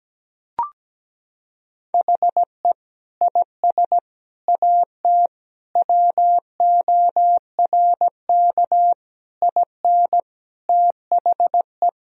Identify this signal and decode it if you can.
Morse code, heard as a keyed tone, about 17 wpm, 700 Hz. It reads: HE IS AT WORK IN THE
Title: Sentences from Top 1000 Words - Review Part 4 - 17wpm